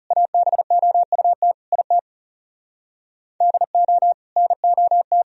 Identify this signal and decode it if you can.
Morse code, heard as a keyed tone, about 40 wpm, 700 Hz. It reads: ABOUT IT DO NOT